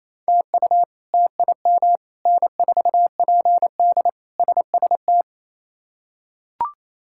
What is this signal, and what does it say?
Morse code, heard as a keyed tone, about 27 wpm, 700 Hz. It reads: TU TIM N4PD SST
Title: K1USN SST QSO Practice - 28wpm